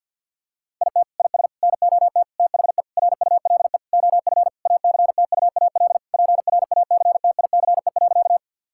Morse code, heard as a keyed tone, about 50 wpm, 700 Hz. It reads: IT IS NOT THE RULE OF ACTUAL PRACTICE.
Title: Sentences from Top 1000 Words - Review Part 5 - 50wpm